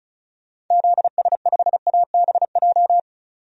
Morse code, heard as a keyed tone, about 36 wpm, 700 Hz. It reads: ZS5ABJ
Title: DXCC-100 Call Signs - 6 Characters - 35wpm